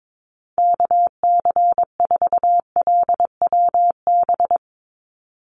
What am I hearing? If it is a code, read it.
Morse code, heard as a keyed tone, about 22 wpm, 700 Hz. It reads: KC4LWB
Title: USA Call Signs 22wpm